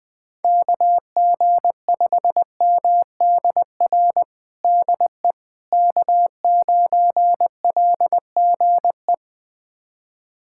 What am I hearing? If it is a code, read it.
Morse code, heard as a keyed tone, about 20 wpm, 700 Hz. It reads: KG5MDR DE K9LGE
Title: USA Callsign DE Callsign 20wpm